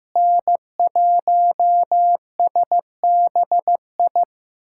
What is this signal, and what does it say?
Morse code, heard as a keyed tone, about 15 wpm, 700 Hz. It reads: N1SBI
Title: USA Callsign DE Callsign 15wpm